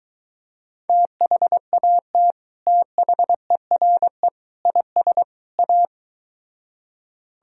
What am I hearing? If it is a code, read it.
Morse code, heard as a keyed tone, about 23 wpm, 700 Hz. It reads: THAT THERE IS A